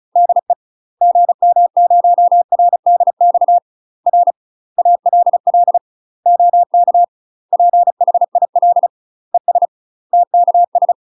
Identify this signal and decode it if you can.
Morse code, heard as a keyed tone, about 36 wpm, 700 Hz. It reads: DE GM0RDX R ALL OK PHIL ES TKS
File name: UK MASTER QSOs TRAINING / 35WPM / UK MASTER QSOs 0009